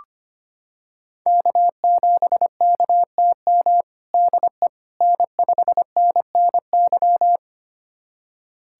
Morse code, heard as a keyed tone, about 25 wpm, 700 Hz. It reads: K7KTM DE N5NNY
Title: USA Callsign DE Callsign 25wpm